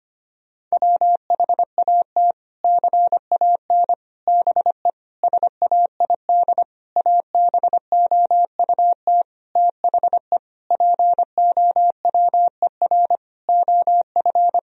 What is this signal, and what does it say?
Morse code, heard as a keyed tone, about 25 wpm, 700 Hz. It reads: WHAT CAN BE SAID ABOUT THE POWER OF
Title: Sentences from Top 1000 Words - Review Part 4 - 25wpm